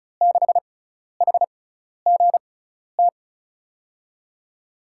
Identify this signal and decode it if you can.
Morse code, heard as a keyed tone, about 36 wpm, spaced as for 12 wpm, 700 Hz. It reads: 6HGT
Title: USA Call Signs - 5 Characters - 6x Character Spacing - 35wpm